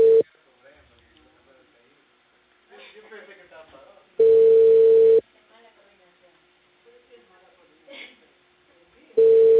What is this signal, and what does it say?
Ringback tone (China, 450 Hz), with speech.